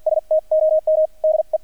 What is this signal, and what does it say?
Morse code, heard as a keyed tone, about 50 wpm, spaced as for 41 wpm, 630 Hz, with radio noise. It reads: STOM DE
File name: archived / CRSnet